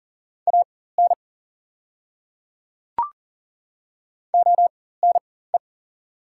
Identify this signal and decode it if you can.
Morse code, heard as a keyed tone, about 39 wpm, spaced as for 19 wpm, 690 Hz. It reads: AN ONE
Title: 4x Character Spacing - Top 100 Words - 40wpm